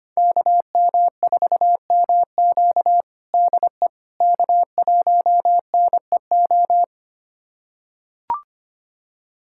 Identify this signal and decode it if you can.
Morse code, heard as a keyed tone, about 25 wpm, 700 Hz. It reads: KM4MQ DE K1NEO
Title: USA Callsign DE Callsign - 5 Characters - 25wpm